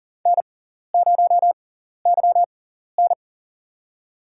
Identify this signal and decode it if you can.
Morse code, heard as a keyed tone, about 41 wpm, spaced as for 14 wpm, 700 Hz. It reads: N0YN
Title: USA Call Signs - 4 Characters - 6x Character Spacing - 40wpm